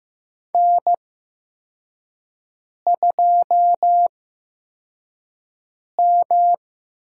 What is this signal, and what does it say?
Morse code, heard as a keyed tone, about 16 wpm, spaced as for 4 wpm, 700 Hz. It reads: N2M
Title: USA Call Signs - 5 Characters - 8x Character Spacing - 15wpm